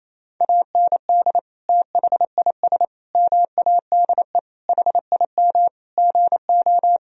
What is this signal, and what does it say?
Morse code, heard as a keyed tone, about 28 wpm, 700 Hz. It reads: AND THIS MADE HIM GO
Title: Sentences from Top 1000 Words - Review Part 1 - 28wpm